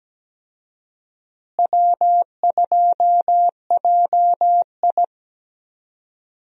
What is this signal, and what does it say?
Morse code, heard as a keyed tone, about 17 wpm, 700 Hz. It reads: W2JI
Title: USA Call Signs - 4 Characters - Repeated 2x - 17wpm